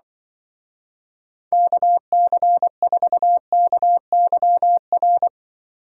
Morse code, heard as a keyed tone, about 23 wpm, 700 Hz. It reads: KC4KYR
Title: Speed Racing - USA 2x3 Call Signs 20wpm